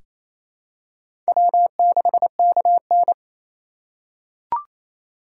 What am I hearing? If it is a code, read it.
Morse code, heard as a keyed tone, about 28 wpm, 700 Hz. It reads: W6KN
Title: USA Call Signs - 4 Characters - 8x Character Spacing - 28wpm